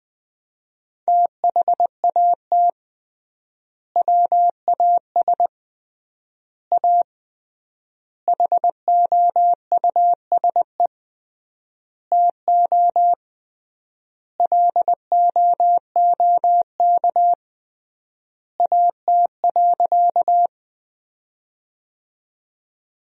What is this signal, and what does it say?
Morse code, heard as a keyed tone, about 21 wpm, 700 Hz. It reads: THAT WAS A HOUSE TO LOOK AT.
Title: 3x Word-Spacing - Sentences from Top 200 Words - 20wpm